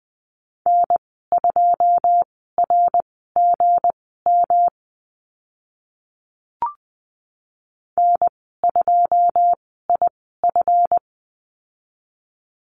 Morse code, heard as a keyed tone, about 20 wpm, spaced as for 14 wpm, 700 Hz. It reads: N2RGM N2IF
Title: 193 Lesson 31g - Letter v - Call Signs 20wpm